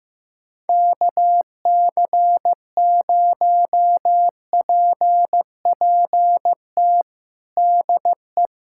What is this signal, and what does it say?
Morse code, heard as a keyed tone, about 15 wpm, 700 Hz. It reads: KC0PPT DE
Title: USA Callsign DE Callsign 15wpm